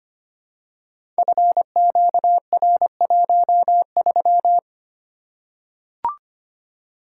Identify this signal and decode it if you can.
Morse code, heard as a keyed tone, about 25 wpm, 700 Hz. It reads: FQR13
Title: Code Groups - 5 Letter:Numbers - 25wpm